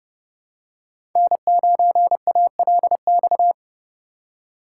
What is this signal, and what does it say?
Morse code, heard as a keyed tone, about 30 wpm, 700 Hz. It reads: N9ALX
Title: USA Call Signs - 5 Characters - 2x Character Spacing - 30wpm